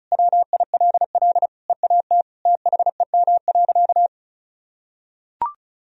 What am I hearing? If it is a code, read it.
Morse code, heard as a keyed tone, about 36 wpm, 700 Hz. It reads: WILL EAT THEM.
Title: Sentences from Top 1000 Words - Review Part 4 - 35wpm